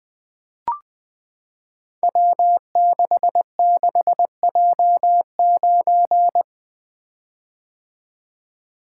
Morse code, heard as a keyed tone, about 21 wpm, 700 Hz. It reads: W66J9
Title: Code Groups - 5 Letter:Numbers - 20wpm